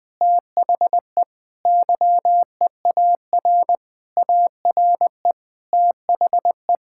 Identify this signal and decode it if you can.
Morse code, heard as a keyed tone, about 20 wpm, 700 Hz. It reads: THE YEAR ARE THE
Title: Sentences with 7-Word Phrases - Spoken Beginning - 20wpm